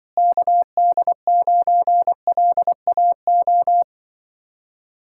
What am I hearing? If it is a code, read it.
Morse code, heard as a keyed tone, about 24 wpm, 700 Hz. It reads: KD9LAO
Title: Speed-Racing Common QSO Phrases - 20wpm